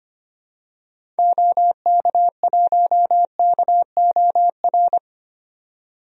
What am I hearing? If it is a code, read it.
Morse code, heard as a keyed tone, about 25 wpm, 700 Hz. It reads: OK1KOR
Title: DXCC-100 Call Signs - 6 Characters - 25wpm